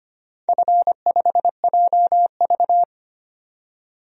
Morse code, heard as a keyed tone, about 25 wpm, 700 Hz. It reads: F5JV
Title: DXCC-100 Call Signs - 4 Characters - 25wpm